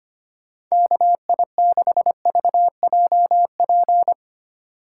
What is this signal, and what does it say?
Morse code, heard as a keyed tone, about 25 wpm, 700 Hz. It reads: KI6VJP
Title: USA Call Signs - 6 Characters - 4x Character Spacing - 25wpm